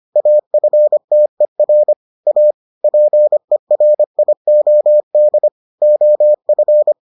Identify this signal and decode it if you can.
Morse code, heard as a keyed tone, about 25 wpm, 600 Hz. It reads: AFTER A PERIOD OF